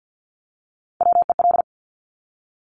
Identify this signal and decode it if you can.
Morse code, heard as a keyed tone, about 50 wpm, 700 Hz. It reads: WEL
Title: Top Three Letter Combinations in Words 50wpm